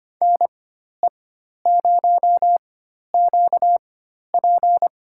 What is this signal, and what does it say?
Morse code, heard as a keyed tone, about 25 wpm, spaced as for 12 wpm, 700 Hz. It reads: NE0QP
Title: USA Call Signs - 4x Character Spacing - 25wpm